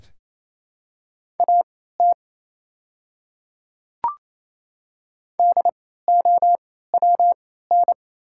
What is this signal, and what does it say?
Morse code, heard as a keyed tone, about 29 wpm, spaced as for 16 wpm, 690 Hz. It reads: AT DOWN
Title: 3x Character Spacing - Top 100 Words - 28wpm